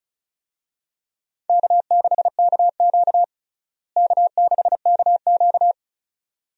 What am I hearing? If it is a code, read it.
Morse code, heard as a keyed tone, about 34 wpm, 700 Hz. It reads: K6KQ K6KQ
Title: USA Call Signs - 4 Characters - Repeated 2x - 35wpm